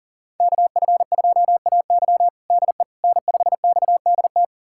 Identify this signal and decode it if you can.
Morse code, heard as a keyed tone, about 40 wpm, 700 Hz. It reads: KF2AY DE N5XDT
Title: USA Callsign DE Callsign - 5 Characters - 40wpm